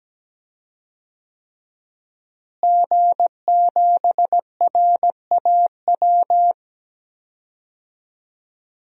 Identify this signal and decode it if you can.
Morse code, heard as a keyed tone, about 17 wpm, 700 Hz. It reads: G7RAW
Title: DXCC-100 Call Signs - 5 Characters - Repeated 3x - 17wpm